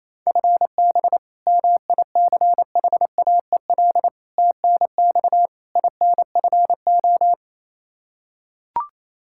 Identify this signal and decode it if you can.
Morse code, heard as a keyed tone, about 28 wpm, 700 Hz. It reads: FB MICHAEL TNX INFO
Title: Practice QSO - Exchange 3 - More Info - 28wpm